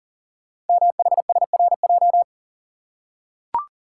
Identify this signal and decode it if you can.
Morse code, heard as a keyed tone, about 42 wpm, 690 Hz. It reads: MHSRJ
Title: Code Groups - 5 Letters - 40wpm